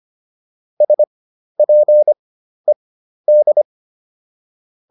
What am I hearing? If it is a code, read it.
Morse code, heard as a keyed tone, about 25 wpm, spaced as for 12 wpm, 600 Hz. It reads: SPED